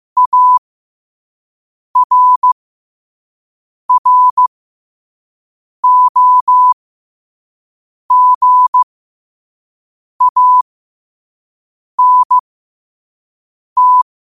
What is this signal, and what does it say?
Morse code, heard as a keyed tone, about 15 wpm, spaced as for 5 wpm, 1000 Hz. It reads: ARROGANT
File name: qotdfiles / 5wpm